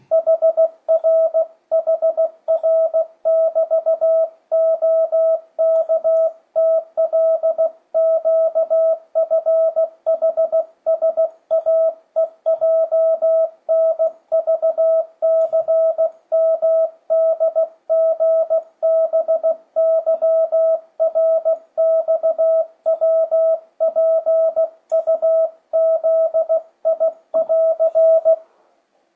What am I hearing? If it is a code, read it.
Morse code, heard as a keyed tone, about 16 wpm, 650 Hz. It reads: HRHR<BT>OKTLQFHSAEJNVCMDGBYRXWPUZI<AR>
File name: 2025cw / speedup / 16wpm